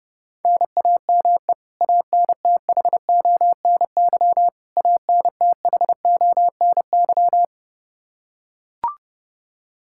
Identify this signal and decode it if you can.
Morse code, heard as a keyed tone, about 30 wpm, 700 Hz. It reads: NAME ANTHONY ANTHONY
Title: Practice QSO - Exchange 1 - Beginning - 30wpm